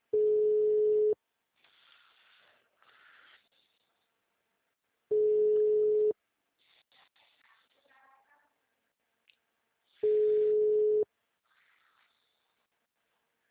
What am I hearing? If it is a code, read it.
Ringback tone (Europe, 425 Hz).
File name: out- -2007- - - .740